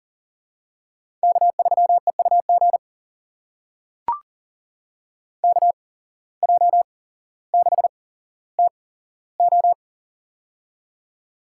Morse code, heard as a keyed tone, about 40 wpm, spaced as for 17 wpm, 700 Hz. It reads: K3EUG KJ6TO
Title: USA Call Signs - 5 Characters - 8x Character Spacing - 40wpm